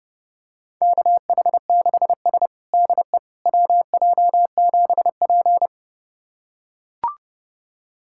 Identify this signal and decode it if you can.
Morse code, heard as a keyed tone, about 30 wpm, 700 Hz. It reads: KH6S DE WJ7P
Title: USA Callsign DE Callsign - 4 Characters - 30wpm